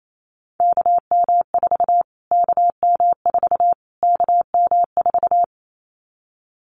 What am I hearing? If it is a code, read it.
Morse code, heard as a keyed tone, about 28 wpm, 700 Hz. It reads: KM4 KM4 KM4
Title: Callsign Prefixes for Top 25 DXCC - Repeated 3x - 28wpm